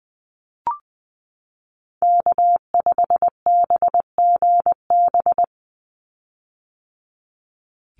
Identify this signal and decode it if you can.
Morse code, heard as a keyed tone, about 20 wpm, 700 Hz. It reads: K5BGB
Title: Call Signs - 20wpm